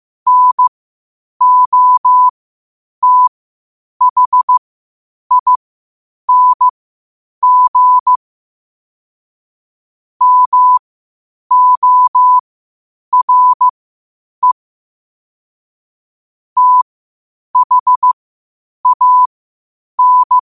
Morse code, heard as a keyed tone, about 15 wpm, spaced as for 8 wpm, 1000 Hz. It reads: NOTHING MORE THAN